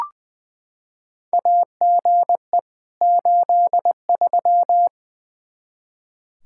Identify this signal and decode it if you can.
Morse code, heard as a keyed tone, about 20 wpm, 700 Hz. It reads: AGE 83
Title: Practice QSO - Exchange 3 - More Info - 20wpm